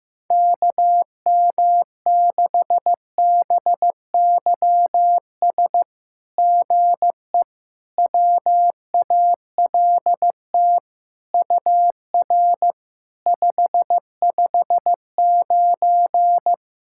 Morse code, heard as a keyed tone, about 15 wpm, 700 Hz. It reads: KM6BYS GE WALT UR 559
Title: SOTA QSO Practice 15wpm